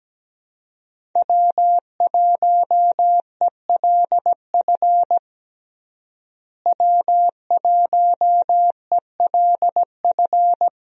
Morse code, heard as a keyed tone, about 17 wpm, 700 Hz. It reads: W1ELF W1ELF
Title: USA Call Signs - 5 Characters - Repeated 2x - 17wpm